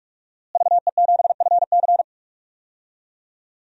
Morse code, heard as a keyed tone, about 43 wpm, 700 Hz. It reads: VE7FC
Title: DXCC-100 Call Signs - 5 Characters - Repeated 2x - 45wpm